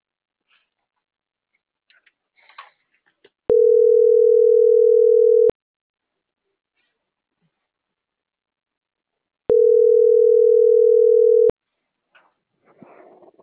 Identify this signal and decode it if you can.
Ringback tone (North America), 440 + 480 Hz.